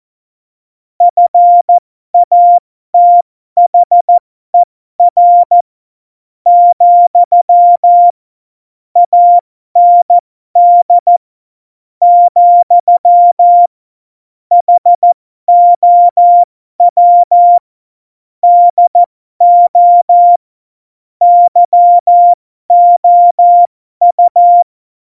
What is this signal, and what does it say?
Morse code, heard as a keyed tone, about 14 wpm, spaced as for 12 wpm, 700 Hz. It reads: FATHER , AND , HOW DO YOU
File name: Morsekurs / Witze